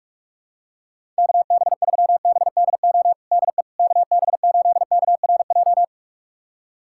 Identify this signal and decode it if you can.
Morse code, heard as a keyed tone, about 46 wpm, 700 Hz. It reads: KB3BDO DE KB8KRJ
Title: USA Callsign DE Callsign 45wpm